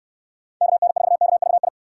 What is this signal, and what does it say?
Morse code, heard as a keyed tone, about 62 wpm, 700 Hz. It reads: KN4KVI